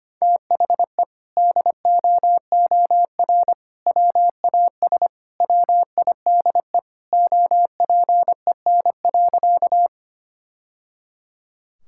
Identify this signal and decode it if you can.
Morse code, heard as a keyed tone, about 25 wpm, 700 Hz. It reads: THE DOOR WAS WIDE OPEN.
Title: Sentences from Top 1000 Words - Review Part 6 - 25wpm